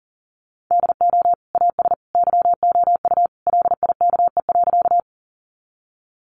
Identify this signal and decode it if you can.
Morse code, heard as a keyed tone, about 40 wpm, 700 Hz. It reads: DO AS YOU LIKE.
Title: Sentences from Top 100 Words 40wpm